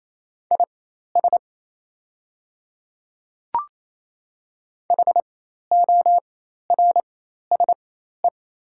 Morse code, heard as a keyed tone, about 28 wpm, spaced as for 13 wpm, 700 Hz. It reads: IS HORSE
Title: 4x Character Spacing - Top 300 Words - 28wpm